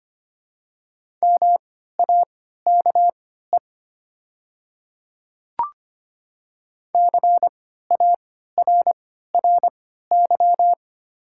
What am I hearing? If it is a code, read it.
Morse code, heard as a keyed tone, about 25 wpm, spaced as for 14 wpm, 690 Hz. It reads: MAKE CARRY
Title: 3x Character Spacing - Top 500 Words - 25wpm